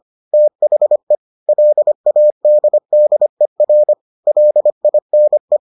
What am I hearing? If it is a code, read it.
Morse code, heard as a keyed tone, about 25 wpm, 600 Hz. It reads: THE LADDER LINE